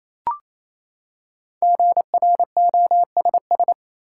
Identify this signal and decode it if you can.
Morse code, heard as a keyed tone, about 28 wpm, 700 Hz. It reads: GROSS
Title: Five Letter Words - 28wpm